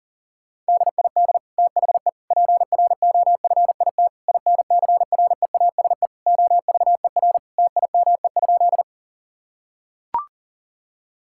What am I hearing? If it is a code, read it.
Morse code, heard as a keyed tone, about 40 wpm, 700 Hz. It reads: DID THE PROFIT INCREASE OVER TIME?
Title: Sentences from Top 1000 Words - Review Part 5 - 40wpm